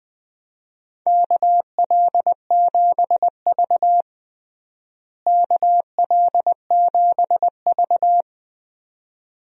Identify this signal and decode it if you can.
Morse code, heard as a keyed tone, about 20 wpm, 700 Hz. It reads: KL7V KL7V
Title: DXCC-100 Call Signs - 4 Characters - Repeated 2x - 20wpm